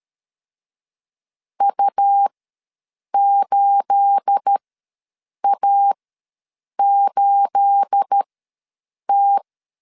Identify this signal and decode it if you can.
Morse code, heard as a keyed tone, about 13 wpm, spaced as for 7 wpm, 800 Hz. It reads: U8A8T
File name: LEZIONI / 9 286 ws 12wpm 3351